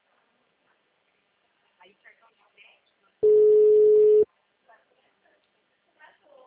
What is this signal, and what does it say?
Ringback tone (Europe, 425 Hz), with speech.